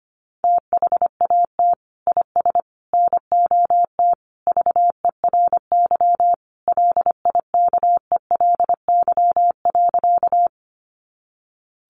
Morse code, heard as a keyed tone, about 25 wpm, 700 Hz. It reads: THAT IS NOT VERY LIKELY.
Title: Sentences from Top 600 Words 25wpm